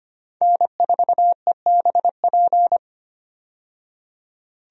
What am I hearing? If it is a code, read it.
Morse code, heard as a keyed tone, about 25 wpm, 700 Hz. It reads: N4EBP
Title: USA Call Signs - 5 Characters - 25wpm